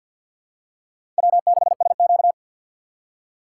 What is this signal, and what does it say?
Morse code, heard as a keyed tone, about 51 wpm, 700 Hz. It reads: W6SQ
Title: USA Call Signs - 4 Characters - 3x Character Spacing - 50wpm